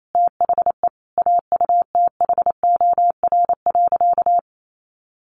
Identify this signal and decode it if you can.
Morse code, heard as a keyed tone, about 28 wpm, 700 Hz. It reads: THE AUTHOR.
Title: Sentences from Top 1200 Words 28wpm